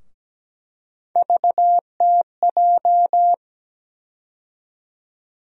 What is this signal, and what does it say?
Morse code, heard as a keyed tone, about 17 wpm, 700 Hz. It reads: VTJ
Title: USA Call Sign Postfixes - 3 Letters 17wpm